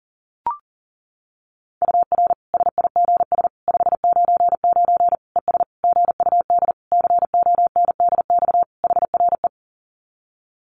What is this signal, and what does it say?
Morse code, heard as a keyed tone, about 40 wpm, 700 Hz. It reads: UR SIGS 599 ES GUD CONDX HRE
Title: Common QSO Phrases 40wpm